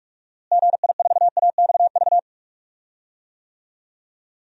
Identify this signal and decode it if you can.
Morse code, heard as a keyed tone, about 46 wpm, 700 Hz. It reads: GI4AXV
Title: DXCC-100 Call Signs - 6 Characters - Repeated 3x - 45wpm